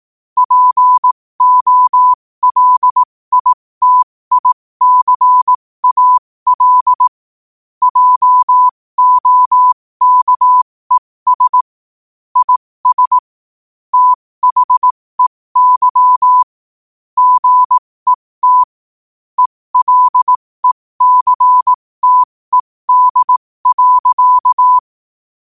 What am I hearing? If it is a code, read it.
Morse code, heard as a keyed tone, about 18 wpm, spaced as for 15 wpm, 1000 Hz. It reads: POLITICAL JOKES IS THEY GET ELECTED.
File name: qotdfiles / 15wpm